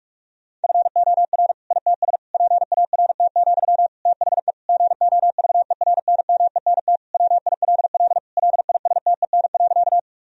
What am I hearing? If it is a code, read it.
Morse code, heard as a keyed tone, about 46 wpm, 700 Hz. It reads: FOR ITS PART, THE GOVERNMENT WILL LISTEN.
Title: Sentences from Top 1000 Words - Review Part 3 - 45wpm